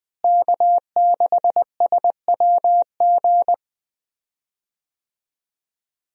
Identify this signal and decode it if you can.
Morse code, heard as a keyed tone, about 20 wpm, 700 Hz. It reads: K6SWG
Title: USA Call Signs - 5 Characters - Repeated 3x - 20wpm